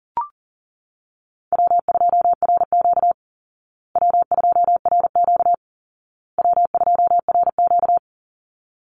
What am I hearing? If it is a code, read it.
Morse code, heard as a keyed tone, about 40 wpm, 700 Hz. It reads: W2RQ W2RQ W2RQ
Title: Easy Call Signs Repeated 3x - 40wpm